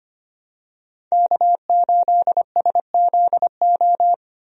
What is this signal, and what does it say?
Morse code, heard as a keyed tone, about 25 wpm, 700 Hz. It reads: K8SZO
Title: USA Call Signs - 5 Characters - 25wpm